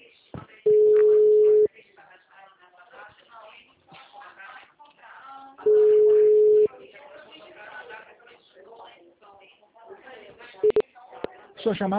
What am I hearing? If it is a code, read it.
Ringback tone (Europe, 425 Hz), with speech.